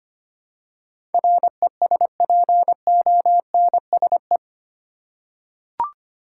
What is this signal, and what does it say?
Morse code, heard as a keyed tone, about 25 wpm, 700 Hz. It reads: RESPONSE
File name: 8 Letter Words - First 3 Letters Given - 25wpm